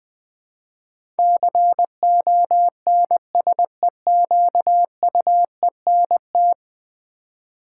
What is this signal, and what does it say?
Morse code, heard as a keyed tone, about 20 wpm, 700 Hz. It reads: CONSEQUENT
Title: Build A Word - 5 Letters At A Time - 12-Letter Words - 20wpm